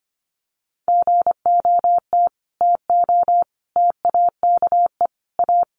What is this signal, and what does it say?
Morse code, heard as a keyed tone, about 25 wpm, 700 Hz. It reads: GOT TO TAKE A
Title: Sets of 5 Words Encore 25wpm